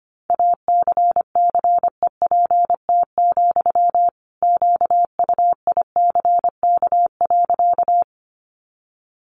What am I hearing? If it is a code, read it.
Morse code, heard as a keyed tone, about 25 wpm, 700 Hz. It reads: ACCEPT, QUICK.
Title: Sentences from Top 1300 Words 25wpm